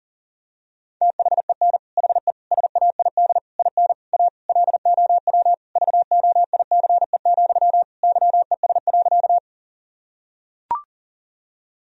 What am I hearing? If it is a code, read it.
Morse code, heard as a keyed tone, about 40 wpm, 700 Hz. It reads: THEN HE SAID IN A LOW VOICE, YES.
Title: Sentences from Top 1000 Words - Review Part 1 - 40wpm